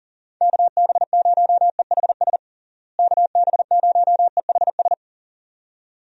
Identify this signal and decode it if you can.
Morse code, heard as a keyed tone, about 40 wpm, 700 Hz. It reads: KB0EHS KB0EHS
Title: USA Call Signs - 6 Characters - Repeated 2x - 40wpm